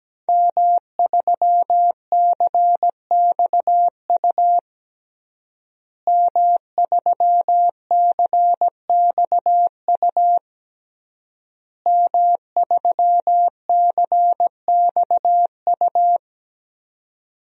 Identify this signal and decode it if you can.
Morse code, heard as a keyed tone, about 17 wpm, 700 Hz. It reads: M3CXU M3CXU M3CXU
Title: DXCC-100 Call Signs - 5 Characters - Repeated 3x - 17wpm